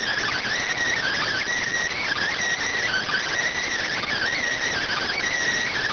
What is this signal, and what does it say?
Emergency Alert System (SAME) tones, heard as harsh buzzing bursts. It is heard from 1.0 to 5.2 s.